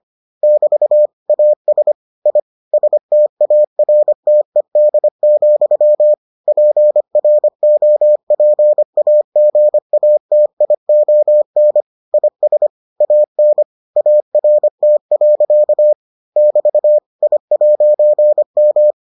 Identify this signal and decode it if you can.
Morse code, heard as a keyed tone, about 25 wpm, 600 Hz. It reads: <BT> AS I STARTED, PROPAGATION IS AN ART. <BT> I'M